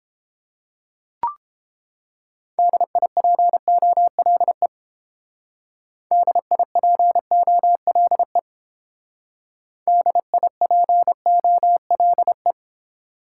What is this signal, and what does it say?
Morse code, heard as a keyed tone, about 30 wpm, 700 Hz. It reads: DIPOLE DIPOLE DIPOLE